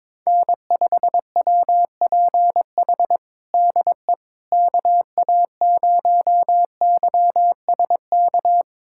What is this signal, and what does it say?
Morse code, heard as a keyed tone, about 22 wpm, 700 Hz. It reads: N5WPH DE KA0YSK
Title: USA Callsign DE Callsign 22wpm